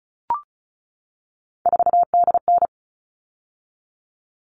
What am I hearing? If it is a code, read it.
Morse code, heard as a keyed tone, about 34 wpm, 700 Hz. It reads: 4DN
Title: All Cut Numbers - 3-Digit 35wpm